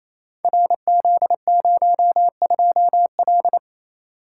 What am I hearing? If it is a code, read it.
Morse code, heard as a keyed tone, about 29 wpm, 700 Hz. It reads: RZ02L
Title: Code Groups - 5 Letter:Numbers - 28wpm